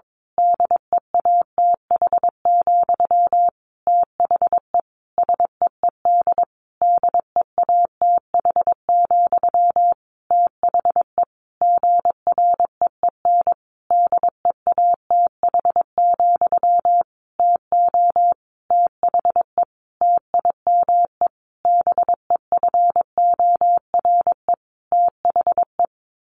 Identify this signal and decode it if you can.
Morse code, heard as a keyed tone, about 22 wpm, 700 Hz. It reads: DEATH, THE SEED DEATH, THE GREEN DEATH, TO THE TIME BEFORE THE
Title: A Sound Of Thunder 22wpm